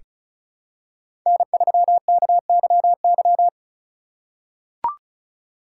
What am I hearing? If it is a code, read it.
Morse code, heard as a keyed tone, about 34 wpm, 700 Hz. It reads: N3KYY
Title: USA Call Signs - 5 Characters - 4x Character Spacing - 35wpm